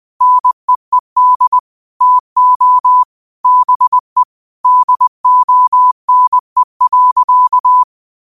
Morse code, heard as a keyed tone, about 20 wpm, 1000 Hz. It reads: NEED TO BE DONE.